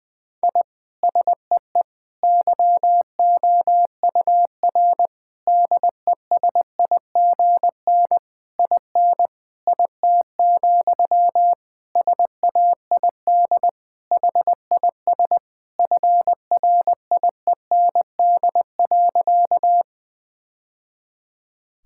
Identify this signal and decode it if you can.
Morse code, heard as a keyed tone, about 20 wpm, 700 Hz. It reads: I SEE YOUR DESIGN IN IT, SAID HIS FRIEND.
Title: Sentences from Top 1000 Words - Review Part 5 - 20wpm